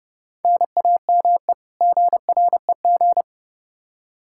Morse code, heard as a keyed tone, about 32 wpm, 700 Hz. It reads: NAME GREG
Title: Practice QSO - Exchange 1 - Beginning - 30wpm